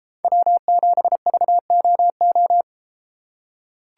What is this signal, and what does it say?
Morse code, heard as a keyed tone, about 32 wpm, 700 Hz. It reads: W7VOO